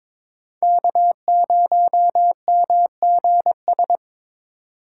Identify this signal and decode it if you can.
Morse code, heard as a keyed tone, about 26 wpm, 700 Hz. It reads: K0MGS
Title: Speed Racing - USA 1x3 Call Signs 22wpm